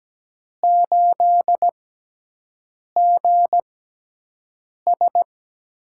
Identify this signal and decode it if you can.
Morse code, heard as a keyed tone, about 17 wpm, spaced as for 6 wpm, 700 Hz. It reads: 8GS
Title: USA Call Signs - 5 Characters - 6x Character Spacing - 17wpm